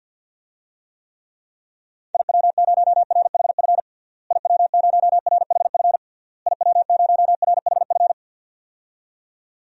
Morse code, heard as a keyed tone, about 47 wpm, 700 Hz. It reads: IW0RHF IW0RHF IW0RHF
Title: DXCC-100 Call Signs - 6 Characters - Repeated 3x - 50wpm